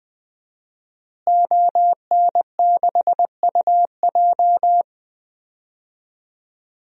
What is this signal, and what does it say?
Morse code, heard as a keyed tone, about 19 wpm, 700 Hz. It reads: ON6UJ
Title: DXCC-100 Call Signs - 5 Characters - Repeated 3x - 20wpm